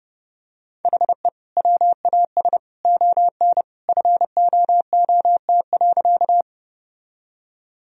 Morse code, heard as a keyed tone, about 30 wpm, 700 Hz. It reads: HE WAS ON FOOT.
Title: Sentences from Top 1000 Words - Review Part 3 - 30wpm